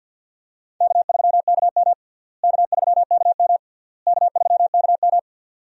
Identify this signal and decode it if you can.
Morse code, heard as a keyed tone, about 52 wpm, 700 Hz. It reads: K3KM K3KM K3KM
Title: USA Call Signs - 4 Characters - Repeated 3x - 50wpm